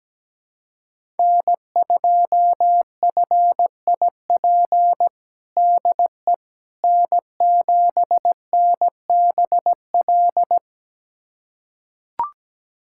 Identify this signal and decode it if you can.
Morse code, heard as a keyed tone, about 17 wpm, 700 Hz. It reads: N2FIP DE N7NBL
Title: USA Callsign DE Callsign - 5 Characters - 17wpm